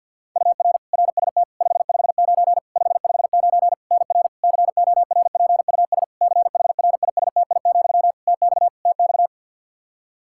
Morse code, heard as a keyed tone, about 52 wpm, 700 Hz. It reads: UR RST 559 559 NR CORPUS CHRISTI, TX TX
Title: K1USN SST QSO Practice 50wpm